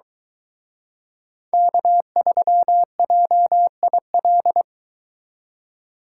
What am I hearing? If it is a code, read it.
Morse code, heard as a keyed tone, about 23 wpm, 700 Hz. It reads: K3JIL